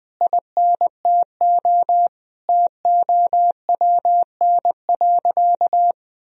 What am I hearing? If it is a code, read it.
Morse code, heard as a keyed tone, about 20 wpm, 700 Hz. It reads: INTO TOWN.
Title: Sentences from Top 1000 Words - Review Part 2 - 20wpm